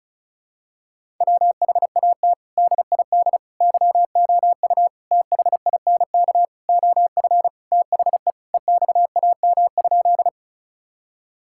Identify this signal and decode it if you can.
Morse code, heard as a keyed tone, about 34 wpm, 700 Hz. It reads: WHAT DID YOU THINK OF THE EXAM?
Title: Sentences from Top 1000 Words - Review Part 5 - 35wpm